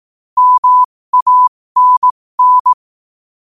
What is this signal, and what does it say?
Morse code, heard as a keyed tone, about 18 wpm, spaced as for 15 wpm, 1000 Hz. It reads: MANN
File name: qotdfiles / 15wpm